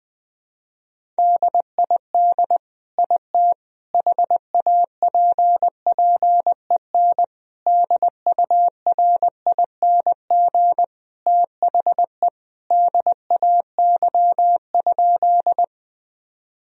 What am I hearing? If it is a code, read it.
Morse code, heard as a keyed tone, about 20 wpm, 700 Hz. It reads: DID IT HAPPEN DURING THE DAY?
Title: Sentences from Top 1000 Words - Review Part 2 - 20wpm